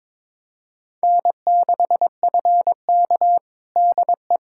Morse code, heard as a keyed tone, about 22 wpm, 700 Hz. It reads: N6FK DE
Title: USA Callsign DE Callsign - 4 Characters - 22wpm